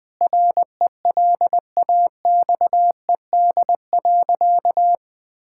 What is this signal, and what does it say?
Morse code, heard as a keyed tone, about 20 wpm, 700 Hz. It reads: RELAXED.
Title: Sentences with 4-Word Phrases - 20wpm